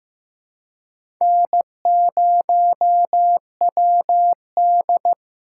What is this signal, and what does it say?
Morse code, heard as a keyed tone, about 15 wpm, 700 Hz. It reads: N0WD
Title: USA Call Signs - 4 Characters - 4x Character Spacing - 15wpm